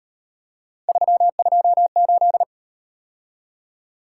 Morse code, heard as a keyed tone, about 37 wpm, 700 Hz. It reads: 328